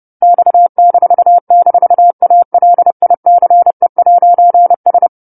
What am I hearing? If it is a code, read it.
Morse code, heard as a keyed tone, about 30 wpm, 700 Hz. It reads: X--ALICE'S